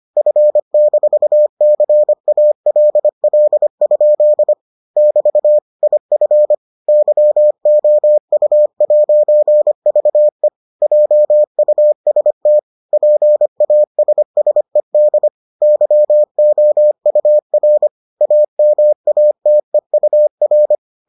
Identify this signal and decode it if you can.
Morse code, heard as a keyed tone, about 25 wpm, 600 Hz. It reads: F-CALL? <BT> IF YOU'VE JUST PASSED YOUR AMATEUR